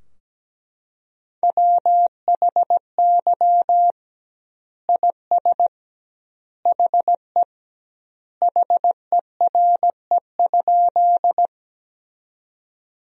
Morse code, heard as a keyed tone, about 17 wpm, 700 Hz. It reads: WHY IS HE HERE?
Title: 2x Word-Spacing - Sentences from Top 200 Words - 17wpm